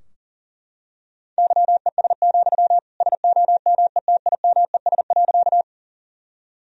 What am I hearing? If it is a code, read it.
Morse code, heard as a keyed tone, about 40 wpm, 700 Hz. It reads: YES, SOMETIMES.
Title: Sentences from Top 1000 Words - Review Part 3 - 40wpm